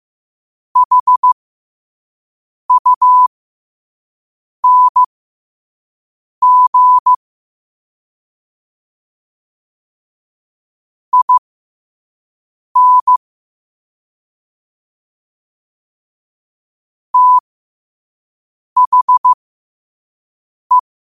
Morse code, heard as a keyed tone, about 15 wpm, spaced as for 5 wpm, 1000 Hz. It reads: HUNG IN THE
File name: qotdfiles / 5wpm